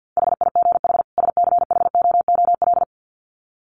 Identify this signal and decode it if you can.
Morse code, heard as a keyed tone, about 50 wpm, 700 Hz. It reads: HIGH SCHOOL
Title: Sets of 2 Words 50wpm